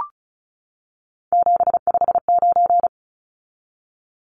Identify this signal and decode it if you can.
Morse code, heard as a keyed tone, about 34 wpm, 700 Hz. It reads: 759
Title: Numbers - 3-Digit 35wpm